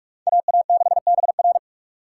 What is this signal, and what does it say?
Morse code, heard as a keyed tone, about 58 wpm, 700 Hz. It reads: AA6BR